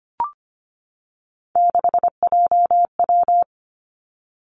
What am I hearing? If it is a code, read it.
Morse code, heard as a keyed tone, about 25 wpm, 700 Hz. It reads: 6JW
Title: Three Challenging Letter-Number Combinations 25wpm